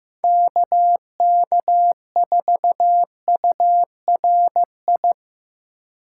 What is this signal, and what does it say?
Morse code, heard as a keyed tone, about 15 wpm, 700 Hz. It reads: KK4URI
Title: USA Call Signs - 6 Characters - 4x Character Spacing - 15wpm